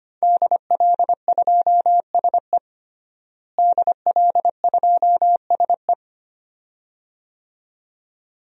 Morse code, heard as a keyed tone, about 25 wpm, 700 Hz. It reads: DL2SE DL2SE
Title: DXCC-100 Call Signs - 5 Characters - Repeated 2x - 25wpm